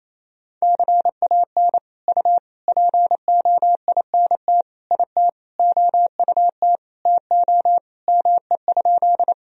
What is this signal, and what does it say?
Morse code, heard as a keyed tone, about 28 wpm, 700 Hz. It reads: CAN U POINT IT OUT TO ME?
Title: Sentences from Top 200 Words and Abbreviations 28wpm